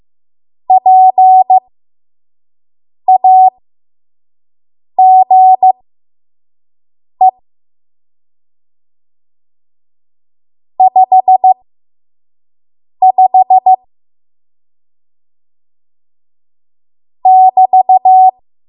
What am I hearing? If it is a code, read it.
Morse code, heard as a keyed tone, about 15 wpm, spaced as for 5 wpm, 750 Hz. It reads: PAGE 55 <BT>
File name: Morse / 05WPM